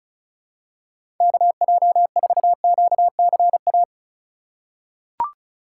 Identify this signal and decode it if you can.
Morse code, heard as a keyed tone, about 34 wpm, 700 Hz. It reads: KJ4QCA
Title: USA Call Signs - 6 Characters - 2x Character Spacing - 35wpm